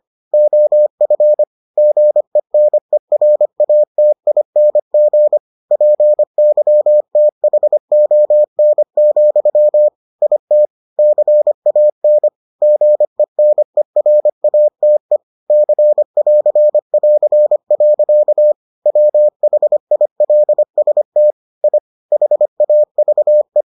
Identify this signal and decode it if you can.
Morse code, heard as a keyed tone, about 25 wpm, 600 Hz. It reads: OF GENERATING PYTHON, IT CAN GENERATE C<AR><AR>. WHILST I HAVE